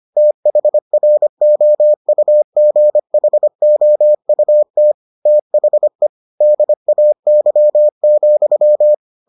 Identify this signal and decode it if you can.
Morse code, heard as a keyed tone, about 25 wpm, 600 Hz. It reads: THROUGHOUT THE DAY,